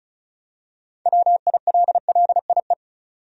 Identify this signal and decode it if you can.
Morse code, heard as a keyed tone, about 34 wpm, 700 Hz. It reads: WILLIE
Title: USA Common Names - 35wpm